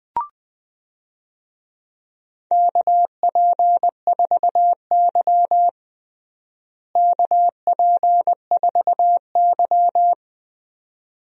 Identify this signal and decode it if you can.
Morse code, heard as a keyed tone, about 20 wpm, 700 Hz. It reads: KP4Y KP4Y
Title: DXCC-100 Call Signs - 4 Characters - Repeated 2x - 20wpm